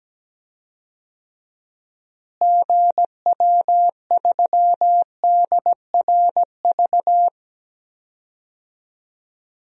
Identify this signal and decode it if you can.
Morse code, heard as a keyed tone, about 17 wpm, 700 Hz. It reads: GW3DRV
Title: DXCC-100 Call Signs - 6 Characters - Repeated 3x - 17wpm